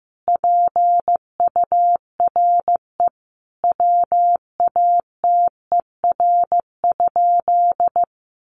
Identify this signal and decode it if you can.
Morse code, heard as a keyed tone, about 15 wpm, 700 Hz. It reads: PURE WATER?
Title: Sentences from Top 1500 Words 15wpm